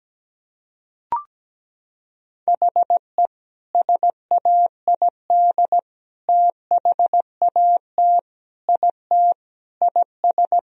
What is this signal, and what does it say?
Morse code, heard as a keyed tone, about 17 wpm, 700 Hz. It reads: HE SAID THAT IT IS
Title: Sentences from Top 1000 Words - Review Part 4 - 17wpm